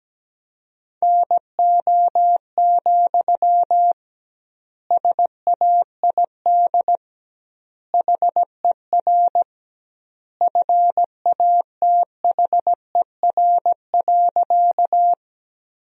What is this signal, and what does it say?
Morse code, heard as a keyed tone, about 17 wpm, 700 Hz. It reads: NO, SAID HER FATHER.
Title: 2x Word-Spacing - Sentences from Top 300 Words - 17wpm